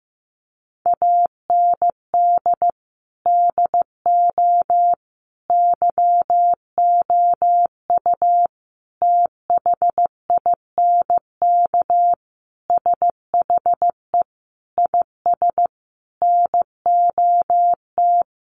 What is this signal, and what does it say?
Morse code, heard as a keyed tone, about 14 wpm, 700 Hz. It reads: AND DO YOU THINK SHE IS NOT
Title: Sentences from Top 600 Words 15wpm